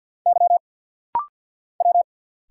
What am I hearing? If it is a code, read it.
Morse code, heard as a keyed tone, about 50 wpm, 700 Hz. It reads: Y W
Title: Single Letters - Rapid-Fire 50wpm